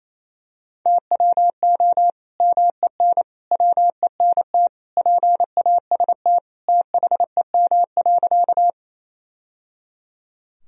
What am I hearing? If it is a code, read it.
Morse code, heard as a keyed tone, about 28 wpm, 700 Hz. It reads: TWO MEN WENT PAST THEM.
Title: Sentences from Top 1000 Words - Review Part 3 - 28wpm